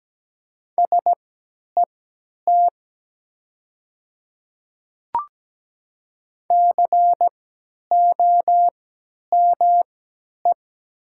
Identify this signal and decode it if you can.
Morse code, heard as a keyed tone, about 17 wpm, spaced as for 10 wpm, 700 Hz. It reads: SET COME
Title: 3x Character Spacing - Top 300 Words - 17wpm